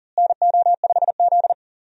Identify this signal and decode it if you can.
Morse code, heard as a keyed tone, about 40 wpm, 700 Hz. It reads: NO5Z
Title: USA Call Signs - 4 Characters - Repeated 3x - 40wpm